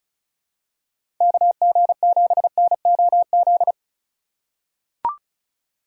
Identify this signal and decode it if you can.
Morse code, heard as a keyed tone, about 34 wpm, 700 Hz. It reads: KG7NOZ
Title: USA Call Signs - 6 Characters - 2x Character Spacing - 35wpm